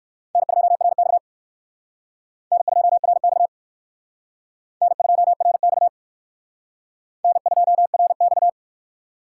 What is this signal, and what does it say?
Morse code, heard as a keyed tone, about 56 wpm, 700 Hz. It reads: N2RX N2RX N2RX N2RX
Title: Speed Racing - USA 1x2 Call Signs 45wpm